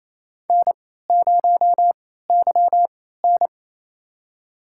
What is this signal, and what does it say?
Morse code, heard as a keyed tone, about 28 wpm, spaced as for 16 wpm, 700 Hz. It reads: N0YN
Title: USA Call Signs - 3x Character Spacing - 28wpm